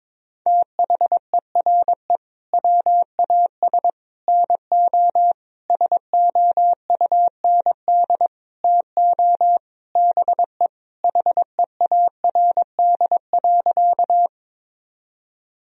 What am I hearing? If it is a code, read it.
Morse code, heard as a keyed tone, about 22 wpm, 700 Hz. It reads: THERE WAS NO SOUND TO BE HEARD.
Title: Sentences from Top 1000 Words - Review Part 1 - 22wpm